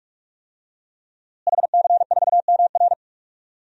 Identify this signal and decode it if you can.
Morse code, heard as a keyed tone, about 46 wpm, 700 Hz. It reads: HC4MR
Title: DXCC-100 Call Signs - 5 Characters - Repeated 2x - 45wpm